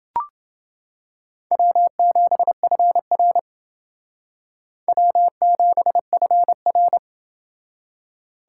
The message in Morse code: W7FR W7FR

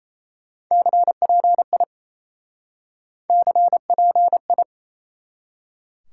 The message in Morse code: CPI CPI

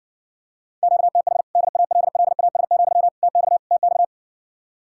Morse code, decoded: QTH DALLAS, TX TX

